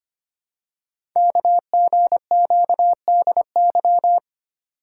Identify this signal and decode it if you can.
Morse code, heard as a keyed tone, about 25 wpm, 700 Hz. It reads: KGQDY
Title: Code Groups - 5 Letters - 25wpm